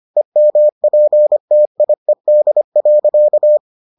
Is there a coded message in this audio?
Yes: Morse code from 0.2 to 3.6 s, heard as a keyed tone, about 25 wpm, 600 Hz. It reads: EMPTIED.